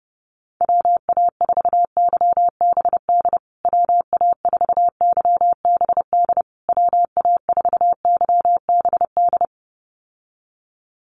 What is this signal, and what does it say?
Morse code, heard as a keyed tone, about 30 wpm, 700 Hz. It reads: WA4YBD WA4YBD WA4YBD
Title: USA Call Signs Repeated 3x 30wpm